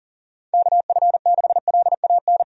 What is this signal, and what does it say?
Morse code, heard as a keyed tone, about 40 wpm, 700 Hz. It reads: KF6LAN
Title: USA Call Signs - 6 Characters - 40wpm